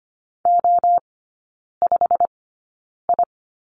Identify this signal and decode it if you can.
Morse code, heard as a keyed tone, about 25 wpm, spaced as for 9 wpm, 700 Hz. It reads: O5I
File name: Lesson 15e - Number 5 - Three Characters 25wpm